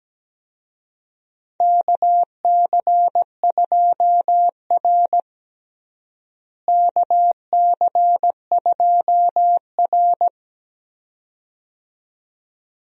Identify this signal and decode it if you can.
Morse code, heard as a keyed tone, about 17 wpm, 700 Hz. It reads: KC2R KC2R
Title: USA Call Signs - 4 Characters - Repeated 2x - 17wpm